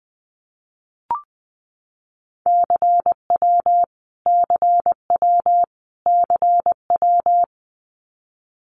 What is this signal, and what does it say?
Morse code, heard as a keyed tone, about 20 wpm, 700 Hz. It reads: CW CW CW
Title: Abbreviations Repeated 3x - 20wpm